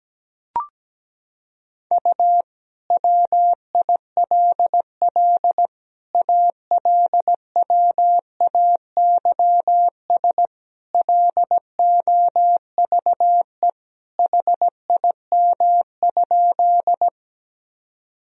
Morse code, heard as a keyed tone, about 17 wpm, 700 Hz. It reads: U WILL ALWAYS LOVE HIM?
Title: Sentences from Top 400 Words and Abbreviations 17wpm